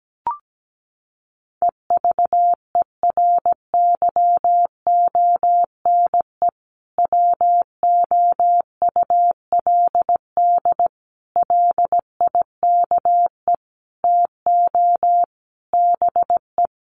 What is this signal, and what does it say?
Morse code, heard as a keyed tone, about 17 wpm, 700 Hz. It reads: EVERYONE WOULD LIKE TO BE
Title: Sentences from Top 1500 Words 17wpm